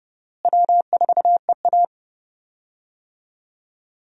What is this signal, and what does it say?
Morse code, heard as a keyed tone, about 31 wpm, 700 Hz. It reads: W4EA
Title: USA Call Signs - 4 Characters - Repeated 3x - 30wpm